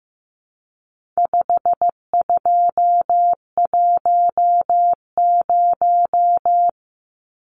Morse code, heard as a keyed tone, about 15 wpm, 700 Hz. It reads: 5210
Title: Numbers - 4-Digit 15wpm